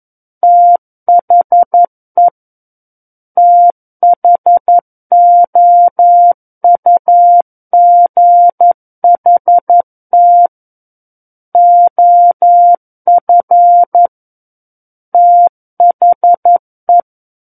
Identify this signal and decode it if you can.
Morse code, heard as a keyed tone, about 11 wpm, 700 Hz. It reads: THE THOUGHT OF THE
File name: 11 WPM